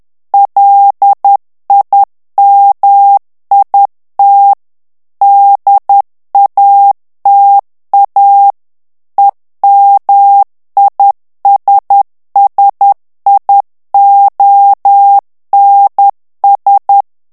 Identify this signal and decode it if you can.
Morse code, heard as a keyed tone, about 10 wpm, 800 Hz. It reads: LIMIT DATA EMISSIONS
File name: ANGLAIS texte clair 3 vitesse 10 mots minute